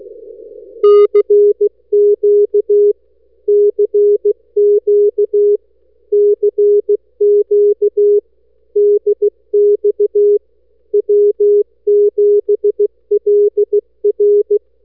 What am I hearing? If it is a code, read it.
Morse code, heard as a keyed tone, about 16 wpm, 410 Hz. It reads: CQ CQ CQ DX W7LR
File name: n6tr / w7lr cq 15-Mar-2009